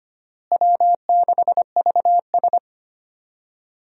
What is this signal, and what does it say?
Morse code, heard as a keyed tone, about 25 wpm, 700 Hz. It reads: W6VS
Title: USA Call Signs - 4 Characters - Repeated 3x - 25wpm